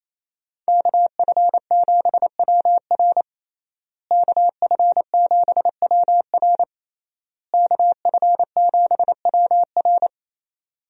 Morse code, heard as a keyed tone, about 28 wpm, 700 Hz. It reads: KF7WR KF7WR KF7WR
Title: USA Call Signs - 5 Characters - Repeated 3x - 28wpm